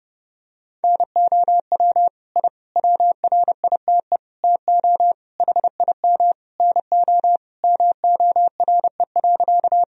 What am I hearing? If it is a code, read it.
Morse code, heard as a keyed tone, about 30 wpm, 700 Hz. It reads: NOW I WRITE TO HIM NO MORE.
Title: Sentences from Top 1000 Words - Review Part 1 - 30wpm